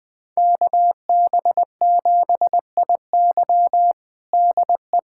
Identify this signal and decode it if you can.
Morse code, heard as a keyed tone, about 19 wpm, 700 Hz. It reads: KB7IY DE
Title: USA Callsign DE Callsign - 5 Characters - 20wpm